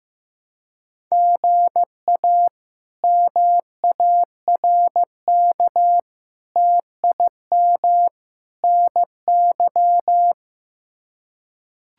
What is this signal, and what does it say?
Morse code, heard as a keyed tone, about 15 wpm, 700 Hz. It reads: GA MARK TIM NY
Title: K1USN SST QSO Practice - 15wpm